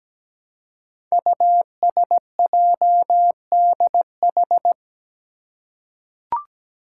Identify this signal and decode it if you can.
Morse code, heard as a keyed tone, about 17 wpm, 700 Hz. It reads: USJDH
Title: Code Groups - 5 Letters - 17wpm